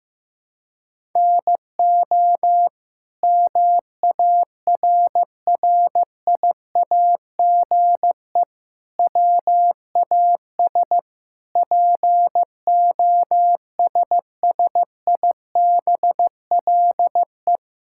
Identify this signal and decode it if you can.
Morse code, heard as a keyed tone, about 15 wpm, 700 Hz. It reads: NO MARRIAGE WAS POSSIBLE
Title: Sentences from Top 1000 Words - Review Part 5 - 15wpm